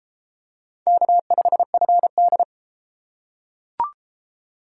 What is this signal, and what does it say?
Morse code, heard as a keyed tone, about 30 wpm, 700 Hz. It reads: K5FD